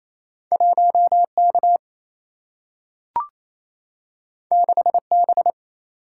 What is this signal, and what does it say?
Morse code, heard as a keyed tone, about 28 wpm, 700 Hz. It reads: 1K 6B
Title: Single-Letter Single-Number 28wpm